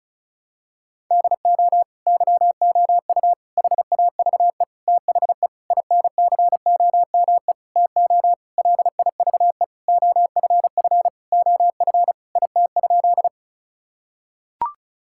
Morse code, heard as a keyed tone, about 34 wpm, 700 Hz. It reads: DO YOU HAVE THE INCOME TO LIVE OFF OF IT?
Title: Sentences from Top 1000 Words - Review Part 5 - 35wpm